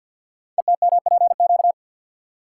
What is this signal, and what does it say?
Morse code, heard as a keyed tone, about 48 wpm, 700 Hz. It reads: ETMPQ
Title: Code Groups - 5 Letters - 50wpm